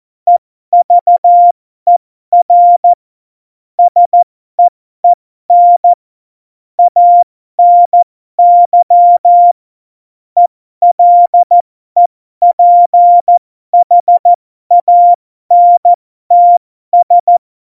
Morse code, heard as a keyed tone, about 14 wpm, spaced as for 12 wpm, 700 Hz. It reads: EVER SEEN ANY ELEPHANTS
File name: Morsekurs / Witze